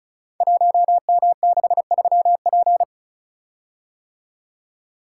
Morse code, heard as a keyed tone, about 36 wpm, 700 Hz. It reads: 1M63P
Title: Code Groups - 5 Letter:Numbers - 35wpm